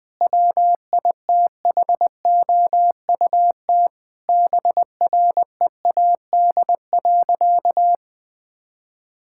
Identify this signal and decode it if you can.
Morse code, heard as a keyed tone, about 20 wpm, 700 Hz. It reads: WITHOUT BREAD.
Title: Sentences from Top 1600 Words 20wpm